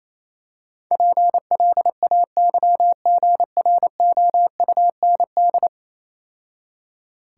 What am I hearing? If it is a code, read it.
Morse code, heard as a keyed tone, about 27 wpm, 700 Hz. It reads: PLAYGROUND
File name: Easy Words - Speed-Racing with Hints - 22wpm